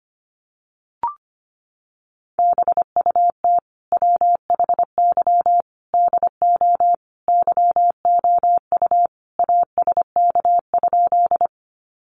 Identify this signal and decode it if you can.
Morse code, heard as a keyed tone, about 25 wpm, 700 Hz. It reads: BUT WHY DO YOU ASK?
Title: Sentences from Top 200 Words 25wpm